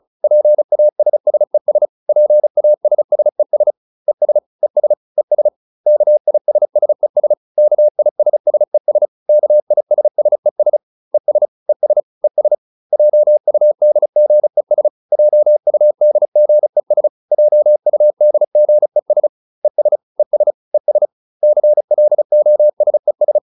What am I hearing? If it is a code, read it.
Morse code, heard as a keyed tone, about 34 wpm, 600 Hz. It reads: PASSES PASSES ES ES ES KISSES KISSES KISSES ES ES ES JUDGES JUDGES JUDGES ES ES ES CLOSES